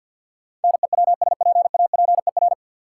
Morse code, heard as a keyed tone, about 53 wpm, 700 Hz. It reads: NEWSPAPER
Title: Sentences from Top 1100 Words - New Word Drill Alt - 50wpm